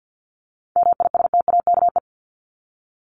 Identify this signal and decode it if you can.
Morse code, heard as a keyed tone, about 48 wpm, 700 Hz. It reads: MISTAKE
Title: Top 1501-1750 Words 50wpm